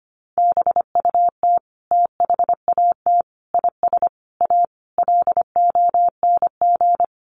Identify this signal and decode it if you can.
Morse code, heard as a keyed tone, about 25 wpm, 700 Hz. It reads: BUT THAT IS A LONG
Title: Sentences from Top 600 Words 25wpm